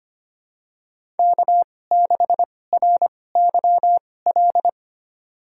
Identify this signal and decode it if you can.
Morse code, heard as a keyed tone, about 25 wpm, spaced as for 18 wpm, 700 Hz. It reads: K6RYL
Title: USA Call Signs - 2x Character Spacing - 25wpm